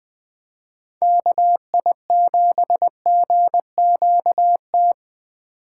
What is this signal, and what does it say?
Morse code, heard as a keyed tone, about 20 wpm, 700 Hz. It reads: KI7GQT
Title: USA Call Signs - 2x Character Spacing - 20wpm